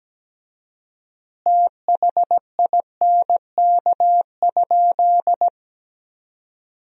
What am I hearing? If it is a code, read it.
Morse code, heard as a keyed tone, about 17 wpm, 700 Hz. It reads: THINK?
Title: 4x Word-Spacing - Sentences from Top 200 Words - 17wpm